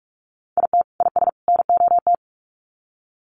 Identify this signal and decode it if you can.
Morse code, heard as a keyed tone, about 44 wpm, 700 Hz. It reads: IT IS NOT
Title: Sets of 3 Words 45wpm